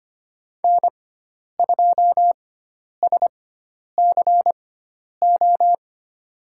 Morse code, heard as a keyed tone, about 25 wpm, spaced as for 10 wpm, 700 Hz. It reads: N2SCO
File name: Lesson 16g - Letter c - Call Signs 25wpm